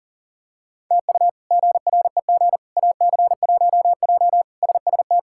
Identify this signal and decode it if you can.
Morse code, heard as a keyed tone, about 37 wpm, 700 Hz. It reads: TU GREG AC1J SST
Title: K1USN SST QSO Practice - 40wpm